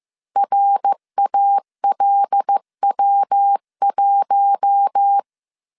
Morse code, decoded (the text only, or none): RALW1